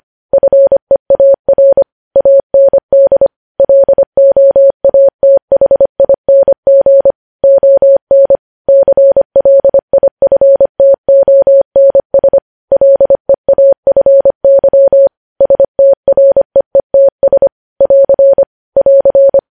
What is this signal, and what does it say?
Morse code, heard as a keyed tone, about 25 wpm, 550 Hz. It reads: FEAR AND LOATHING ON CLIFTONS LEAFY STREETS <AR> <AR>
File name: podcast / News headlines-1x-25x25